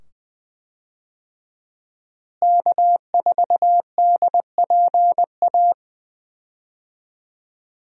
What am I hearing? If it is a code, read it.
Morse code, heard as a keyed tone, about 20 wpm, 700 Hz. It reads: K4DPA
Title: USA Call Signs - 5 Characters - Repeated 2x - 20wpm